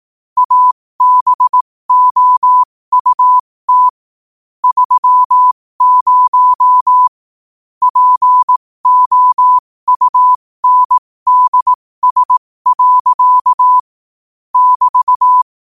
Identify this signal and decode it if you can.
Morse code, heard as a keyed tone, about 18 wpm, spaced as for 15 wpm, 1000 Hz. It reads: ABOUT 30 POUNDS. <BT>
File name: qotdfiles / 15wpm